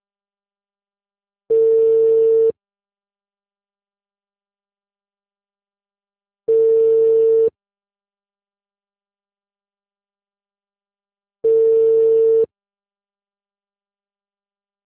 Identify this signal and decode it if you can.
Ringback tone (China, 450 Hz).